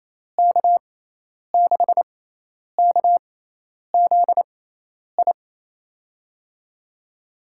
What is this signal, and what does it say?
Morse code, heard as a keyed tone, about 28 wpm, spaced as for 10 wpm, 700 Hz. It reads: K6KZI
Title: USA Call Signs - 6x Character Spacing - 28wpm